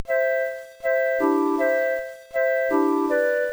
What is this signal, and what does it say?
There is no code.